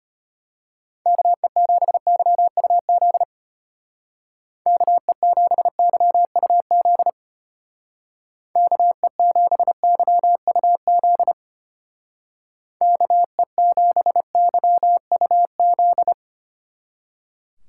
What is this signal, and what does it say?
Morse code, heard as a keyed tone, about 32 wpm, 700 Hz. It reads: KE7YUZ KE7YUZ KE7YUZ KE7YUZ